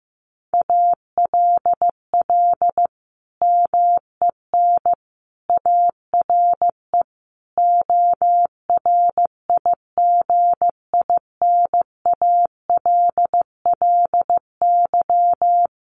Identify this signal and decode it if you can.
Morse code, heard as a keyed tone, about 15 wpm, 700 Hz. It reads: ALL MEN ARE ORIGINALLY
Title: Sentences from Top 1500 Words 15wpm